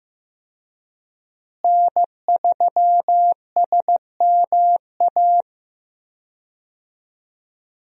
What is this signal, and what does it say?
Morse code, heard as a keyed tone, about 15 wpm, 700 Hz. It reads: N3SMA
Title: USA Call Signs - 5 Characters - Repeated 2x - 15wpm